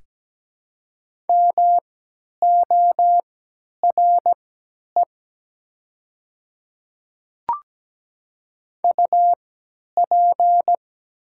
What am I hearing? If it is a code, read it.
Morse code, heard as a keyed tone, about 17 wpm, spaced as for 10 wpm, 700 Hz. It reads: MORE UP
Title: 3x Character Spacing - Top 100 Words - 17wpm